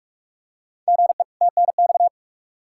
Morse code, heard as a keyed tone, about 46 wpm, 700 Hz. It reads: GE TNX
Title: Practice QSO - Exchange 1 - Beginning - 45wpm